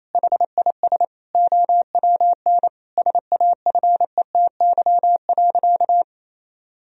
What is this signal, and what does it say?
Morse code, heard as a keyed tone, about 28 wpm, 700 Hz. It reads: HIS OWN SAFETY.
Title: Sentences from Top 1000 Words - Review Part 4 - 28wpm